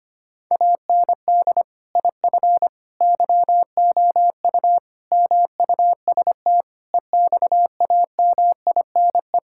Morse code, heard as a keyed tone, about 25 wpm, 700 Hz. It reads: AND IF YOU MUST EXAMINE
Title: Sentences from Top 1600 Words 25wpm